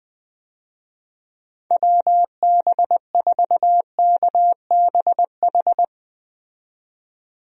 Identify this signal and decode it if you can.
Morse code, heard as a keyed tone, about 20 wpm, 700 Hz. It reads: WB4KBH
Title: USA Call Signs - 6 Characters - 20wpm